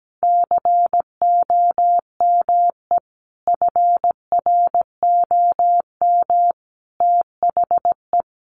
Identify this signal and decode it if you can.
Morse code, heard as a keyed tone, about 17 wpm, 700 Hz. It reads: COME FROM THE
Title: Sentences from Top 1400 Words 17wpm